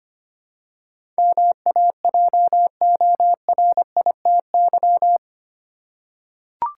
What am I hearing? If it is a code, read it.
Morse code, heard as a keyed tone, about 25 wpm, 700 Hz. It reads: MAJORITY
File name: 8 Letter Words - First 3 Letters Given - 25wpm